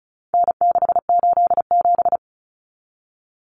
Morse code, heard as a keyed tone, about 34 wpm, 700 Hz. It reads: N687
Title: Common Cut Numbers - 4-Digit 35wpm